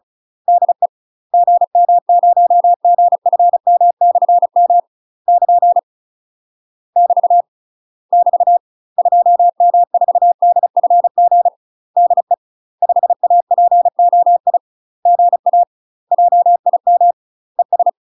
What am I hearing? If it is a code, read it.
Morse code, heard as a keyed tone, about 34 wpm, 700 Hz. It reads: DE GM0GFM/M <KN> <BT> <BT> 2M4DFG DE 5APOI GA JIM ES
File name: UK MASTER QSOs TRAINING / 35WPM / UK MASTER QSOs 0042